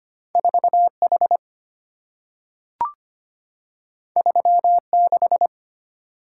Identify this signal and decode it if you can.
Morse code, heard as a keyed tone, about 25 wpm, 700 Hz. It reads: 4H 36
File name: Two Letter-Number Combinations - 25wpm